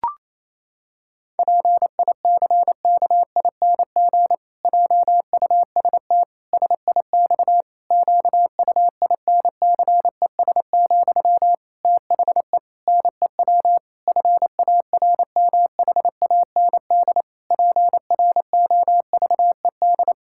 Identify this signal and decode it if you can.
Morse code, heard as a keyed tone, about 28 wpm, 700 Hz. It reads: PICKING JUST SIX QUINCES, THE NEW FARMHAND PROVED
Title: Pangram Challenge v2 - 28wpm